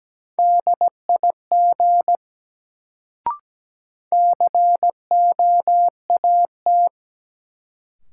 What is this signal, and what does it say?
Morse code, heard as a keyed tone, about 17 wpm, 700 Hz. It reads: DIG COAT
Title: Sentences from Top 1600 Words - New Word Drill - 17wpm